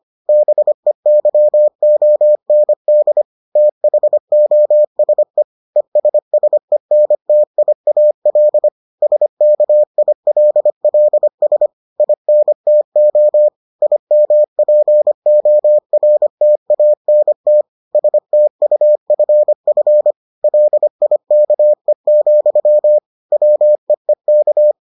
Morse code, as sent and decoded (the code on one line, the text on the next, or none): -... -.-- --- -. -.. / - .... --- ... / . ... ... -. - .. .- .-.. / ... -.- .. .-.. .-.. ... / .. -. - --- / .. -- .--. --- .-. - .- -. - / ... - ..- ..-. ..-. / .-.. .. -.- . --..-- / .-- . -.-
BEYOND THOSE ESSENTIAL SKILLS INTO IMPORTANT STUFF LIKE, WEEK